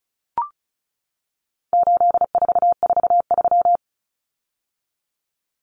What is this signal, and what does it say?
Morse code, heard as a keyed tone, about 34 wpm, 700 Hz. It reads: 8443
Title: Common Cut Numbers - 4-Digit 35wpm